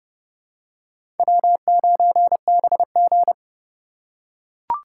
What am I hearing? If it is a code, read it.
Morse code, heard as a keyed tone, about 30 wpm, 700 Hz. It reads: W9BG
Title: USA Call Signs - 4 Characters - 4x Character Spacing - 30wpm